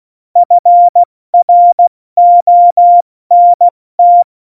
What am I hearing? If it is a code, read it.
Morse code, heard as a keyed tone, about 16 wpm, 700 Hz. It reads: FRONT
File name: Morsekurs / Witze / Witz 04 August 2020, 70 F80